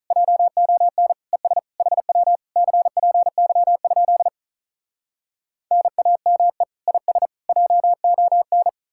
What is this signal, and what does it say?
Morse code, heard as a keyed tone, about 47 wpm, 700 Hz. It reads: JON ES HW CPY? NAME IS JON